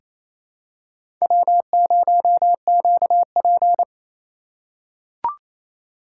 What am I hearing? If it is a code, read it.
Morse code, heard as a keyed tone, about 28 wpm, 700 Hz. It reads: W0QP
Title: USA Call Signs - 4 Characters - 2x Character Spacing - 28wpm